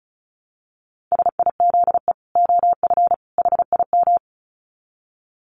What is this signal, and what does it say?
Morse code, heard as a keyed tone, about 34 wpm, 700 Hz. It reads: SIZE OF HIM
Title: Dit Challenge 35wpm